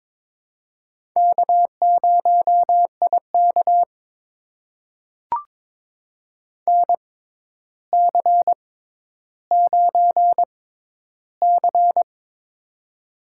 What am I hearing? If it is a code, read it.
Morse code, heard as a keyed tone, about 22 wpm, spaced as for 11 wpm, 700 Hz. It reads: K0IK NC9C
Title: USA Call Signs - 4 Characters - 6x Character Spacing - 22wpm